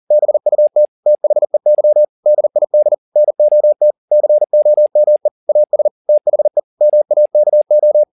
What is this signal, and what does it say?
Morse code, heard as a keyed tone, about 40 wpm, 600 Hz. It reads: BUT THEY DID NOT COME AS THE MAKO
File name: the old man and the sea 2 40wpm